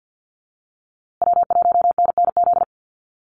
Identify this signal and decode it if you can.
Morse code, heard as a keyed tone, about 50 wpm, 700 Hz. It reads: W1NNZ
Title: USA Call Signs Repeated 3x 50wpm